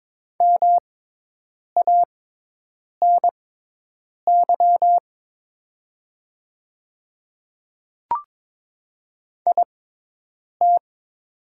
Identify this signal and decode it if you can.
Morse code, heard as a keyed tone, about 22 wpm, spaced as for 8 wpm, 700 Hz. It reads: MANY IT
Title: 6x Character Spacing - Top 300 Words - 22wpm